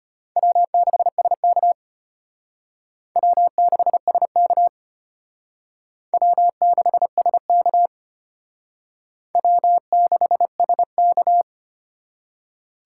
Morse code, heard as a keyed tone, about 32 wpm, 700 Hz. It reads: W6SK W6SK W6SK W6SK